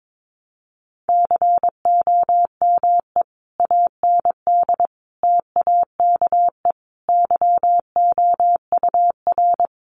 Morse code, heard as a keyed tone, about 22 wpm, 700 Hz. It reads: COME AND TAKE YOUR
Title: Sentences from Top 1500 Words 22wpm